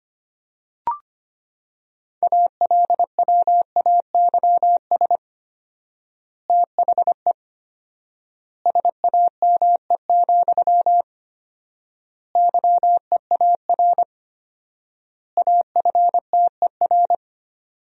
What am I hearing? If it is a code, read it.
Morse code, heard as a keyed tone, about 25 wpm, 700 Hz. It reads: ALWAYS THE SAME, YEAR AFTER
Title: 4x Word-Spacing - Sentences from Top 300 Words - 25wpm